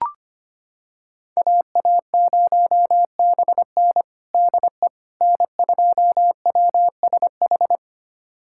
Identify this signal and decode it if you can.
Morse code, heard as a keyed tone, about 25 wpm, 700 Hz. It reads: AA0BN DE N2WSH
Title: USA Callsign DE Callsign - 5 Characters - 25wpm